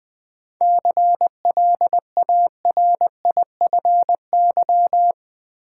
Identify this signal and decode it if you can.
Morse code, heard as a keyed tone, about 20 wpm, 700 Hz. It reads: CLARIFY
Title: 7 Letter Words - First 3 Letters Given - 20wpm